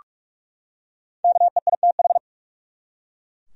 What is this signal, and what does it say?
Morse code, heard as a keyed tone, about 43 wpm, 700 Hz. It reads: KEITH
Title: USA Common Names - 45wpm